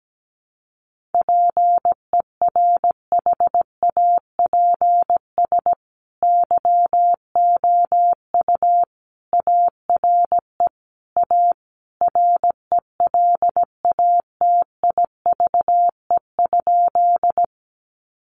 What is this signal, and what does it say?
Morse code, heard as a keyed tone, about 17 wpm, 700 Hz. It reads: PERHAPS YOU ARE A RELATIVE?
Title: Sentences from Top 1400 Words 17wpm